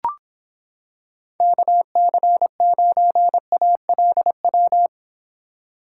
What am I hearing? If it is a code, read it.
Morse code, heard as a keyed tone, about 26 wpm, 700 Hz. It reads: KC9ALW